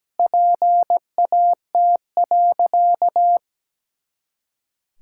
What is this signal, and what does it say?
Morse code, heard as a keyed tone, about 17 wpm, 700 Hz. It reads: PAT.
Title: Pangram Challenge v2 - 17wpm